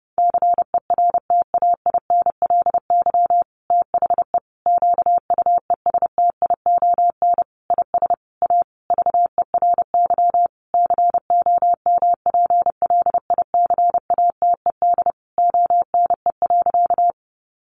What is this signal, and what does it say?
Morse code, heard as a keyed tone, about 30 wpm, 700 Hz. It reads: CERTAINLY THE QUESTION IS A VERY COMPLICATED ONE.
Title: Sentences from Top 1200 Words 30wpm